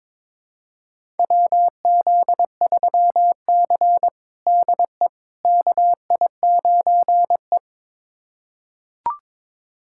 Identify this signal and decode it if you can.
Morse code, heard as a keyed tone, about 22 wpm, 700 Hz. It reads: WZ3C DE KI9E
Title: USA Callsign DE Callsign - 4 Characters - 22wpm